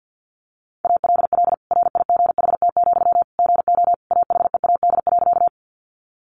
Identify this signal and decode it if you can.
Morse code, heard as a keyed tone, about 50 wpm, 700 Hz. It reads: ALL RIGHT, GO AHEAD.
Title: Sentences from Top 1100 Words 50wpm